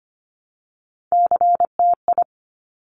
Morse code, heard as a keyed tone, about 25 wpm, 700 Hz. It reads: CTI
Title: Top Three Letter Combinations in Words 25wpm